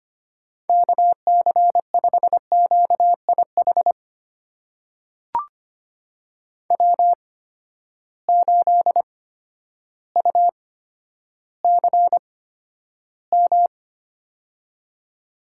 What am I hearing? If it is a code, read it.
Morse code, heard as a keyed tone, about 25 wpm, spaced as for 11 wpm, 700 Hz. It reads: KC5QIH W8UCM
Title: USA Call Signs - 8x Character Spacing - 25wpm